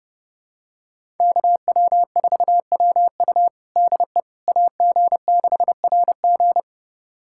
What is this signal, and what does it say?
Morse code, heard as a keyed tone, about 30 wpm, 700 Hz. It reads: KW4WU DE AG6RG
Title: USA Callsign DE Callsign - 5 Characters - 30wpm